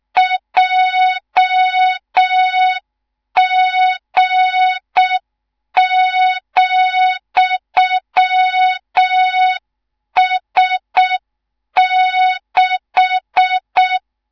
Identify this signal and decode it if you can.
Morse code, heard as a keyed tone, about 6 wpm, 3700 Hz, with noise. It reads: JG,S6
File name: Morze-HA2MN / hang / 18 IRASJELEK2